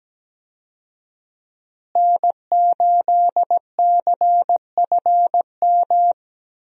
Morse code, heard as a keyed tone, about 17 wpm, 700 Hz. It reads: N8CFM
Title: USA Call Signs - 5 Characters - Repeated 3x - 17wpm